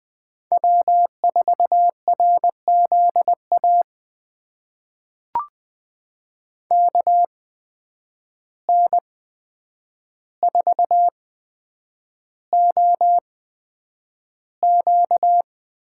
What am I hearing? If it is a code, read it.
Morse code, heard as a keyed tone, about 20 wpm, spaced as for 9 wpm, 700 Hz. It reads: W4RZA KN4OQ
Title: USA Call Signs - 5 Characters - 8x Character Spacing - 20wpm